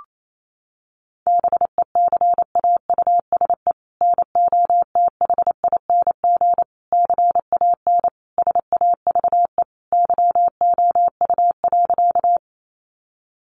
Morse code, heard as a keyed tone, about 28 wpm, 700 Hz. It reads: BECAUSE NOTHING CAN SAVE YOU.
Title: Sentences from Top 1100 Words 28wpm